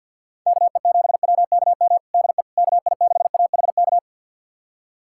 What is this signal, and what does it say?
Morse code, heard as a keyed tone, about 51 wpm, 700 Hz. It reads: KE7WKM DE KI6AHK
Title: USA Callsign DE Callsign - 6 Characters - 50wpm